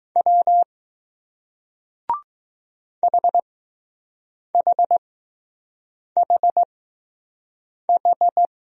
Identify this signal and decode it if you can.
Morse code, heard as a keyed tone, about 20 wpm, spaced as for 6 wpm, 700 Hz. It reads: W HHHH